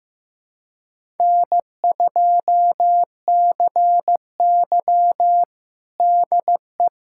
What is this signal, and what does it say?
Morse code, heard as a keyed tone, about 15 wpm, 700 Hz. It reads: N2CY DE
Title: USA Callsign DE Callsign - 4 Characters - 15wpm